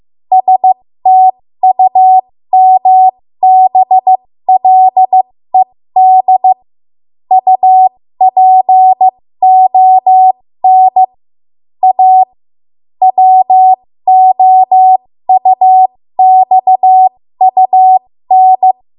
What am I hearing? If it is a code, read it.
Morse code, heard as a keyed tone, about 15 wpm, spaced as for 13 wpm, 750 Hz. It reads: STUMBLED UPON A WOUXUN
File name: Morse / 13WPM